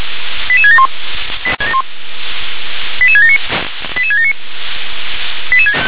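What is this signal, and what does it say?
Five-tone selective call (ZVEI), heard as quick runs of five tones, tones of about 70 ms. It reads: 88671 88679 88679
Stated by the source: Five-tone ZVEI selective call 88775 received by SDR.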